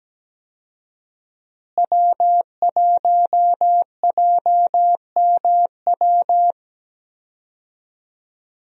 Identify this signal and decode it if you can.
Morse code, heard as a keyed tone, about 17 wpm, 700 Hz. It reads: W1JMW
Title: USA Call Signs - 5 Characters - Repeated 3x - 17wpm